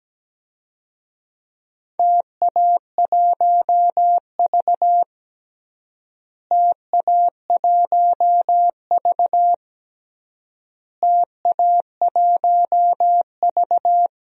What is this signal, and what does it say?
Morse code, heard as a keyed tone, about 17 wpm, 700 Hz. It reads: TA1V TA1V TA1V
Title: DXCC-100 Call Signs - 4 Characters - Repeated 3x - 17wpm